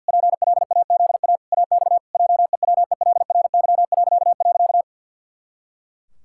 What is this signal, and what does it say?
Morse code, heard as a keyed tone, about 51 wpm, 700 Hz. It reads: PLAZA AX JEWELRY..
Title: Pangram Challenge v2 - 50wpm